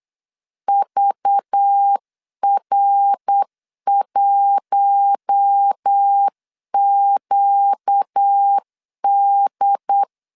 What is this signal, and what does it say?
Morse code, heard as a keyed tone, about 8 wpm, 800 Hz. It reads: VR1QD
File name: CW / DET / 1 6 1500C S C 8wpm 5219